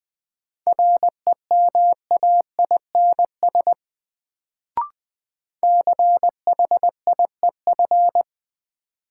Morse code, heard as a keyed tone, about 20 wpm, 700 Hz. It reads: REMAINS CHIEF
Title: Sets of 5 Words Encore - New Word Drill - 20wpm